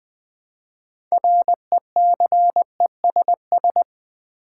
Morse code, heard as a keyed tone, about 20 wpm, 700 Hz. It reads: RECESS
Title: Build A Word - 3 Letters At A Time - 9-Letter Words - 20wpm